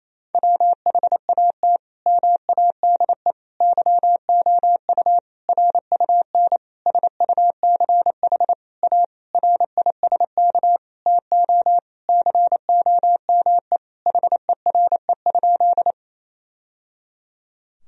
Morse code, heard as a keyed tone, about 27 wpm, 700 Hz. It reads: WHAT MADE YOU RUN SUCH A RISK TO COME HERE?
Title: Sentences from Top 1000 Words - Review Part 4 - 28wpm